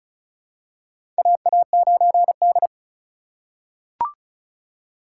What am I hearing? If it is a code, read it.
Morse code, heard as a keyed tone, about 36 wpm, 700 Hz. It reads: AA9D
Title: USA Call Signs - 2x Character Spacing - 35wpm